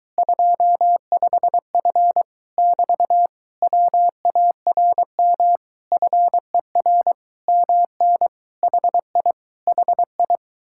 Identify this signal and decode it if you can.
Morse code, heard as a keyed tone, about 23 wpm, 700 Hz. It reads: 25F <BT> WARM FER MN HI HI